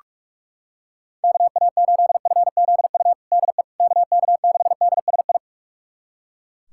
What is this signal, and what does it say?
Morse code, heard as a keyed tone, about 41 wpm, 700 Hz. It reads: KA8FZU DE KK6DSI
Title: USA Callsign DE Callsign - 6 Characters - 45wpm